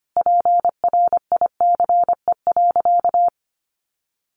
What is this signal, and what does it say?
Morse code, heard as a keyed tone, about 25 wpm, 700 Hz. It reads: PRICE.
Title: Sentences from Top 600 Words 25wpm